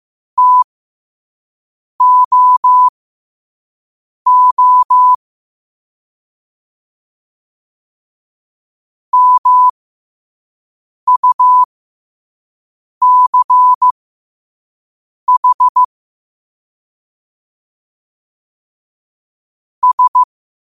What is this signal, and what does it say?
Morse code, heard as a keyed tone, about 15 wpm, spaced as for 5 wpm, 1000 Hz. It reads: TOO MUCH S
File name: qotdfiles / 5wpm